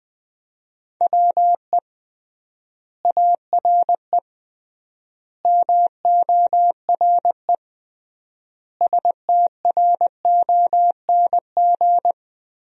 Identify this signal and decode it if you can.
Morse code, heard as a keyed tone, about 20 wpm, 700 Hz. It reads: WE ARE MORE STRONG
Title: 3x Word-Spacing - Sentences from Top 500 Words - 20wpm